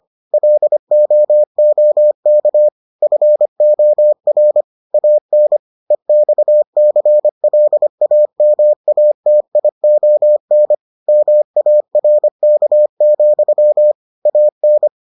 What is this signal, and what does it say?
Morse code, heard as a keyed tone, about 25 wpm, 600 Hz. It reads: LOOK FOR AN EXCLAMATION MARK, AN